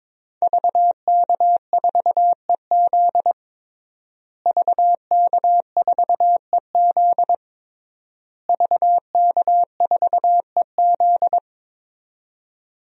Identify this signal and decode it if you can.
Morse code, heard as a keyed tone, about 22 wpm, 700 Hz. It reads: VK4EZ VK4EZ VK4EZ
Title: DXCC-100 Call Signs - 5 Characters - Repeated 3x - 22wpm